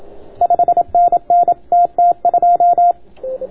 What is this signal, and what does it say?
Morse code, heard as a keyed tone, about 27 wpm, 690 Hz, with radio noise. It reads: 5NN TT2